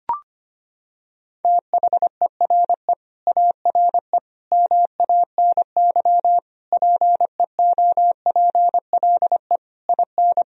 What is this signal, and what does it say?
Morse code, heard as a keyed tone, about 25 wpm, 700 Hz. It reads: THERE ARE MANY PEOPLE IN
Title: Sentences from Top 1000 Words - Review Part 1 - 25wpm